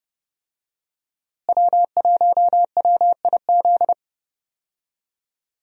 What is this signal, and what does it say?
Morse code, heard as a keyed tone, about 32 wpm, 700 Hz. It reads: W1WIZ
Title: USA Call Signs - 5 Characters - Repeated 3x - 30wpm